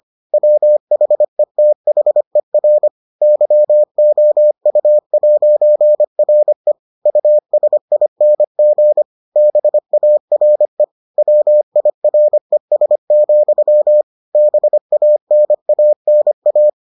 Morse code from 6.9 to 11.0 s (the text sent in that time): USING BARE